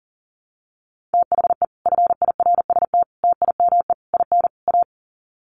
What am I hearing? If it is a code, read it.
Morse code, heard as a keyed tone, about 40 wpm, 700 Hz. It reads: THE FIRST TIME IN A
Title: Sets of 5 Words Encore 40wpm